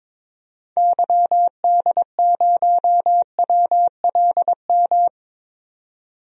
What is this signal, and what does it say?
Morse code, heard as a keyed tone, about 22 wpm, 700 Hz. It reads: YD0WLM
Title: DXCC-100 Call Signs - 6 Characters - 22wpm